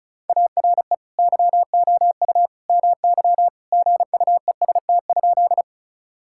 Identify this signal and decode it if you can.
Morse code, heard as a keyed tone, about 36 wpm, 700 Hz. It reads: ARE YOU MY GUEST?
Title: Sentences from Top 1600 Words 35wpm